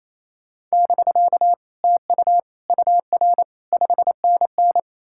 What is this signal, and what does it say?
Morse code, heard as a keyed tone, about 37 wpm, 700 Hz. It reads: <BK> TU UR 5NN